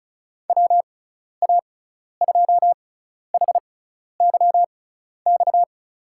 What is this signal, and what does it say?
Morse code, heard as a keyed tone, about 36 wpm, spaced as for 12 wpm, 690 Hz. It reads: WA2HYX
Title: USA Call Signs - 6 Characters - 6x Character Spacing - 35wpm